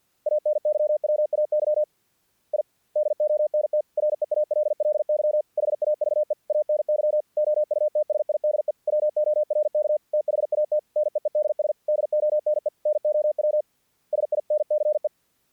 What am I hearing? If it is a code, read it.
Morse code, heard as a keyed tone, about 48 wpm, 600 Hz. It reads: ANYWAY I DONT REALLY HAVE ANY OUTSIDE WORK THAT NEEDS DONE NOW SINCE
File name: archived / CRSnet